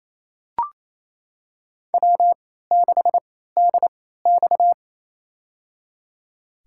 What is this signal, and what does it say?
Morse code, heard as a keyed tone, about 28 wpm, spaced as for 16 wpm, 700 Hz. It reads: W6DX
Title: USA Call Signs - 4 Characters - 3x Character Spacing - 28wpm